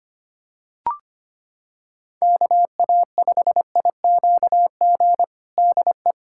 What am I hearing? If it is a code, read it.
Morse code, heard as a keyed tone, about 25 wpm, 700 Hz. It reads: KA5IQG DE
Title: USA Callsign DE Callsign - 6 Characters - 25wpm